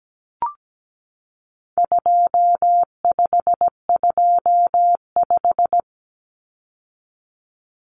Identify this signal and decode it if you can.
Morse code, heard as a keyed tone, about 17 wpm, 700 Hz. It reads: 2525
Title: Common Cut Numbers - 4-Digit 17wpm